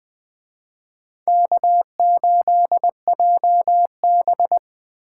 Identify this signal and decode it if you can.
Morse code, heard as a keyed tone, about 20 wpm, 700 Hz. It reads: K8JB
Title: Speed Racing - USA Call Signs 17wpm